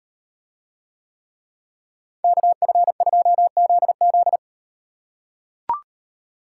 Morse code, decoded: KF2ZZ

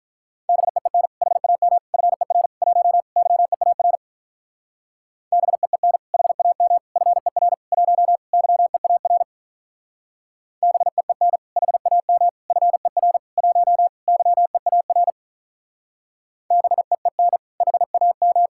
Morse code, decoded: BEEN HAM FER 1 YEAR BEEN HAM FER 1 YEAR BEEN HAM FER 1 YEAR BEEN HAM

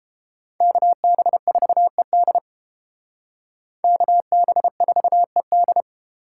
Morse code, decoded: KB4ED KB4ED

